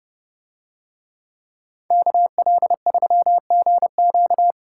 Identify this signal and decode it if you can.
Morse code, heard as a keyed tone, about 29 wpm, 700 Hz. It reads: KL3GQ